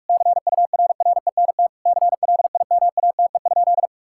Morse code, heard as a keyed tone, about 47 wpm, 700 Hz. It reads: CURRENT CLIMATE?
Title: Sentences from Top 1600 Words 45wpm